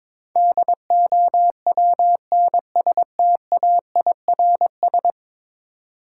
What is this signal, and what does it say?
Morse code, heard as a keyed tone, about 22 wpm, 700 Hz. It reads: DOWNSTAIRS
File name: practice / 22wpm / 10 Letter Words - First 3 Letters Given - 22wpm